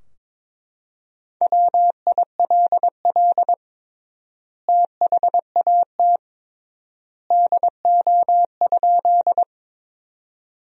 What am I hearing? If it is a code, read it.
Morse code, heard as a keyed tone, about 22 wpm, 700 Hz. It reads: WILL THAT DO?
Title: 3x Word-Spacing - Sentences from Top 100 Words - 22wpm